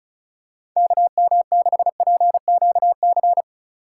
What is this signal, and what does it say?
Morse code, heard as a keyed tone, about 36 wpm, 700 Hz. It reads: KM6PQC
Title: USA Call Signs - 4x Character Spacing - 35wpm